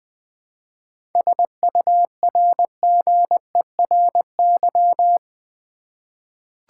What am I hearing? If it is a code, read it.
Morse code, heard as a keyed tone, about 20 wpm, 700 Hz. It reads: SURGERY
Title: 7 Letter Words - First 3 Letters Given - 20wpm